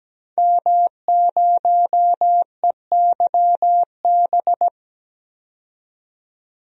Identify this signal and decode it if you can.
Morse code, heard as a keyed tone, about 17 wpm, 700 Hz. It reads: M0EYB
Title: DXCC-100 Call Signs - 5 Characters - 17wpm